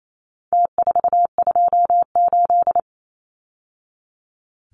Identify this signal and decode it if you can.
Morse code, heard as a keyed tone, about 28 wpm, 700 Hz. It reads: T428
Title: Numbers - Contest Style - 4-Digit 28wpm